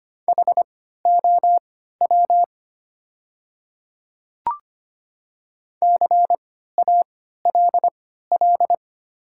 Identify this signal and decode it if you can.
Morse code, heard as a keyed tone, about 25 wpm, spaced as for 14 wpm, 700 Hz. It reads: HOW CALL
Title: 3x Character Spacing - Top 100 Words - 25wpm